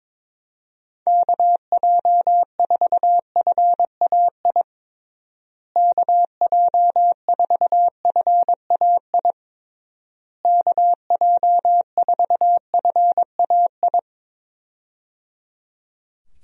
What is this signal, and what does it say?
Morse code, heard as a keyed tone, about 22 wpm, 700 Hz. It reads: KJ4FAI KJ4FAI KJ4FAI
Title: USA Call Signs - 6 Characters - Repeated 3x - 22wpm